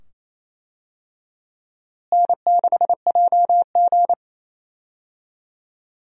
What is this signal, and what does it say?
Morse code, heard as a keyed tone, about 27 wpm, 700 Hz. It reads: N6JG
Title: USA Call Signs - 4 Characters - Repeated 2x - 28wpm